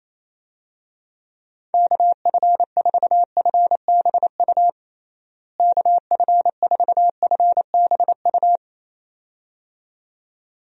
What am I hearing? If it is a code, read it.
Morse code, heard as a keyed tone, about 28 wpm, 700 Hz. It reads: KF4FBU KF4FBU
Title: USA Call Signs - 6 Characters - Repeated 2x - 28wpm